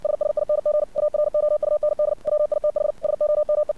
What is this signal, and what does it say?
Morse code, heard as a keyed tone, about 60 wpm, 600 Hz. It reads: SSING AROUND WITH SOME
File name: archived / CRSnet